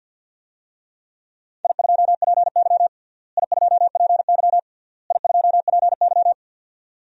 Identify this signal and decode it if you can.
Morse code, heard as a keyed tone, about 53 wpm, 700 Hz. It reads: I2PY I2PY I2PY
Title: DXCC-100 Call Signs - 4 Characters - Repeated 3x - 50wpm